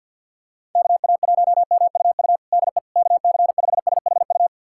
Morse code, heard as a keyed tone, about 51 wpm, 700 Hz. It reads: KA1MUU DE KC5SHU
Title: USA Callsign DE Callsign - 6 Characters - 50wpm